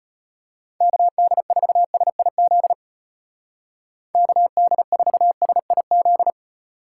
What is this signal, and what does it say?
Morse code, heard as a keyed tone, about 33 wpm, 700 Hz. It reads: KD4SIZ KD4SIZ